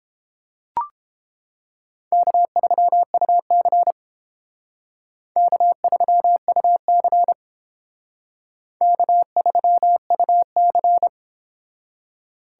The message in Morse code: K3UC K3UC K3UC

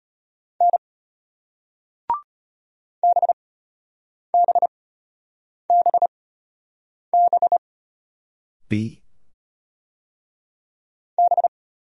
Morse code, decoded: N BBBB B

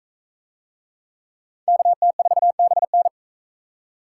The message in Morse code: KT4BN

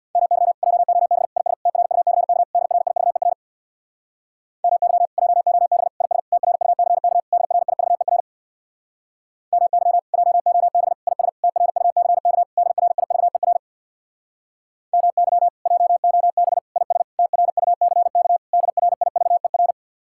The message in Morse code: MY JOB IS TRUCK DRIVER MY JOB IS TRUCK DRIVER MY JOB IS TRUCK DRIVER MY JOB IS TRUCK DRIVER

